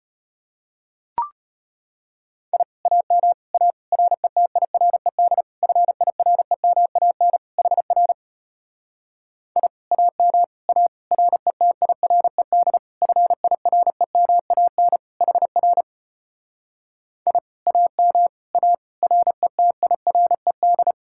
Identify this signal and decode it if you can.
Morse code, heard as a keyed tone, about 33 wpm, 700 Hz. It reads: I AM A RETIRED FIREMAN HR I AM A RETIRED FIREMAN HR I AM A RETIRED